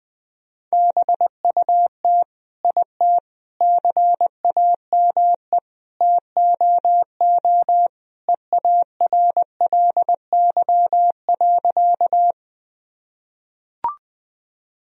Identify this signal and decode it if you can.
Morse code, heard as a keyed tone, about 20 wpm, 700 Hz. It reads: BUT IT CAME TOO EARLY.
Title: Sentences from Top 1000 Words - Review Part 2 - 20wpm